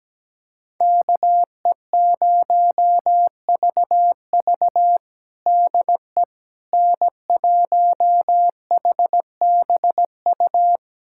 Morse code, heard as a keyed tone, about 17 wpm, 700 Hz. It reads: KE0VV DE N1HBU
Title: USA Callsign DE Callsign - 5 Characters - 17wpm